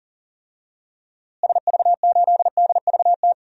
Morse code, heard as a keyed tone, about 40 wpm, 700 Hz. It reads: SV8DVT
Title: DXCC-100 Call Signs - 6 Characters - Repeated 3x - 40wpm